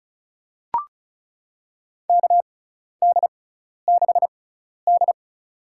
Morse code, heard as a keyed tone, about 35 wpm, spaced as for 12 wpm, 690 Hz. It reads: KD6D
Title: USA Call Signs - 6 Characters - 6x Character Spacing - 35wpm